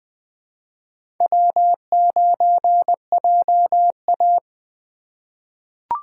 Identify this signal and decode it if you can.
Morse code, heard as a keyed tone, about 20 wpm, 700 Hz. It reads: W9JA
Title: USA Call Signs - 3x Character Spacing - 20wpm